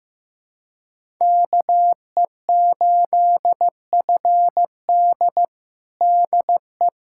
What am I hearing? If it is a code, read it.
Morse code, heard as a keyed tone, about 15 wpm, 700 Hz. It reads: KE8FD DE
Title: USA Callsign DE Callsign - 5 Characters - 15wpm